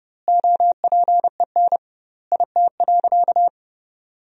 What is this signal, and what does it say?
Morse code, heard as a keyed tone, about 30 wpm, 700 Hz. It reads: OPEN IT.
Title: 2x Word-Spacing - Sentences from Top 300 Words - 30wpm